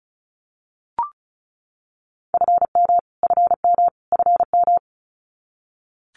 Morse code, heard as a keyed tone, about 34 wpm, 700 Hz. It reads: FM FM FM
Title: Abbreviations Repeated 3x - 35wpm